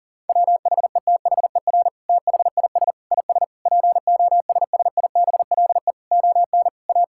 Morse code, heard as a keyed tone, about 37 wpm, 700 Hz. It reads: WHETHER THIS IS POSSIBLE ON A
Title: Sentences from Top 1000 Words - Review Part 5 - 40wpm